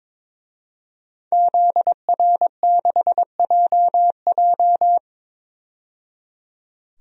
Morse code, heard as a keyed tone, about 22 wpm, 700 Hz. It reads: ZR6JJ
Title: DXCC-100 Call Signs - 5 Characters - 22wpm